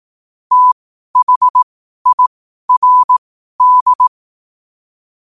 Morse code, heard as a keyed tone, about 18 wpm, spaced as for 12 wpm, 1000 Hz. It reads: THIRD
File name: qotdfiles / 12wpm